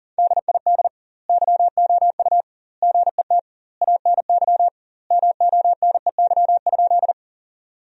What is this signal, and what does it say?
Morse code, heard as a keyed tone, about 40 wpm, 700 Hz. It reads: DID YOU GET ANY MONEY?
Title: 2x Word-Spacing - Sentences from Top 400 Words - 40wpm